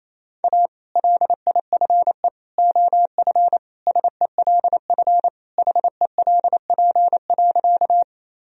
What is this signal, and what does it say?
Morse code, heard as a keyed tone, about 28 wpm, 700 Hz. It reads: A LIFE OF SELF HELP.
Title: Sentences from Top 1000 Words - Review Part 1 - 28wpm